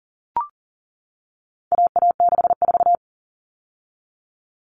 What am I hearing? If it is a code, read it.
Morse code, heard as a keyed tone, about 40 wpm, 700 Hz. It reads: AA64
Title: All Cut Numbers - 4-Digit 40wpm